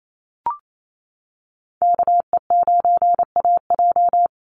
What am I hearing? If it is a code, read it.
Morse code, heard as a keyed tone, about 28 wpm, 700 Hz. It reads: KE9AJ
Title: NA SOTA Chaser Callsigns 28wpm